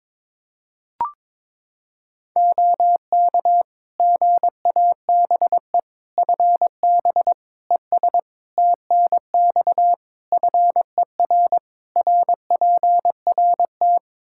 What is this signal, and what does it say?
Morse code, heard as a keyed tone, about 21 wpm, 700 Hz. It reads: OK GABE FB ES TNX FER RPRT
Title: Practice QSO - Exchange 2 - Info - 22wpm